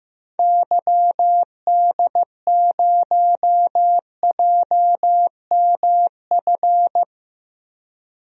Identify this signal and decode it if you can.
Morse code, heard as a keyed tone, about 15 wpm, 700 Hz. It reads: YD0JMF
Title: DXCC-100 Call Signs - 6 Characters - Repeated 3x - 15wpm